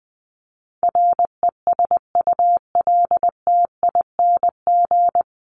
Morse code, heard as a keyed tone, about 20 wpm, 700 Hz. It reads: RESULTING
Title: ING Suffix - Top 100 Words with Base Words 20wpm